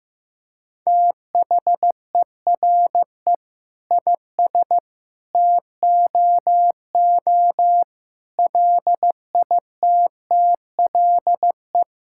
Morse code, heard as a keyed tone, about 15 wpm, 700 Hz. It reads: THERE IS TOO LITTLE
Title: Sentences from Top 1000 Words - Review Part 5 - 15wpm